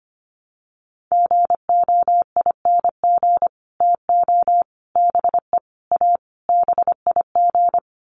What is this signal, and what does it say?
Morse code, heard as a keyed tone, about 25 wpm, 700 Hz. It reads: GOING TO BE A BIG
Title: Sets of 5 Words Encore2 25wpm